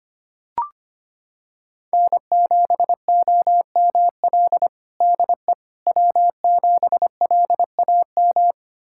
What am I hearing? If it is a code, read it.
Morse code, heard as a keyed tone, about 25 wpm, 700 Hz. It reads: N7OML DE W7LAM
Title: USA Callsign DE Callsign - 5 Characters - 25wpm